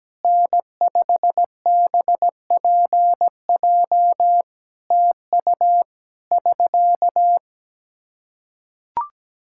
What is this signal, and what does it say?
Morse code, heard as a keyed tone, about 17 wpm, 700 Hz. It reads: N5BPJ TU <SK>
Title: K1USN SST QSO Practice 17wpm